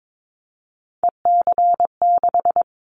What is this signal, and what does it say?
Morse code, heard as a keyed tone, about 22 wpm, 700 Hz. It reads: EC6
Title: Callsign Prefixes for Top 100 DXCC - 22wpm